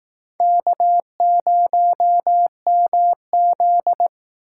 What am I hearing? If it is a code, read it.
Morse code, heard as a keyed tone, about 18 wpm, 700 Hz. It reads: K0MZ
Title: Speed Racing - USA Call Signs 15wpm